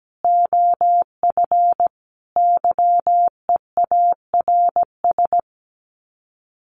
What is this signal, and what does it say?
Morse code, heard as a keyed tone, about 17 wpm, 700 Hz. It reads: OF YEARS
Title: Sets of 5 Words Encore 17wpm